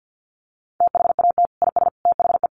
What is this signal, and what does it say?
Morse code, heard as a keyed tone, about 50 wpm, 700 Hz. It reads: THAT IS THE
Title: Sentences from Top 1300 Words 50wpm